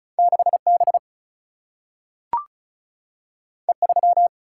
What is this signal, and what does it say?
Morse code, heard as a keyed tone, about 36 wpm, 700 Hz. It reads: 6B E3
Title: Single-Letter Single-Number 35wpm